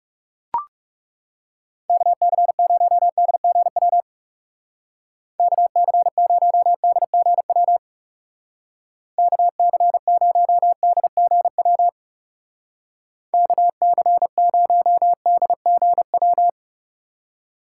Morse code, heard as a keyed tone, about 38 wpm, 700 Hz. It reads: KC0DGW KC0DGW KC0DGW KC0DGW